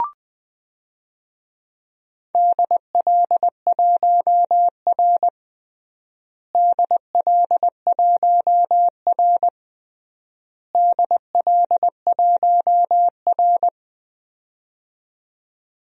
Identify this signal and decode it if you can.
Morse code, heard as a keyed tone, about 20 wpm, 700 Hz. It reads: DL1R DL1R DL1R
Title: DXCC-100 Call Signs - 4 Characters - Repeated 3x - 20wpm